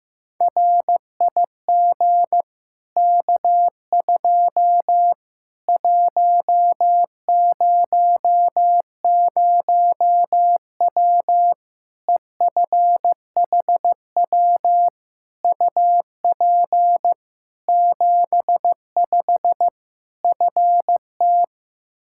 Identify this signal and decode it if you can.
Morse code, heard as a keyed tone, about 15 wpm, 700 Hz. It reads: RIG K2 100W EFHW UP 75 FT
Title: Practice QSO - Exchange 2 - Info - 15wpm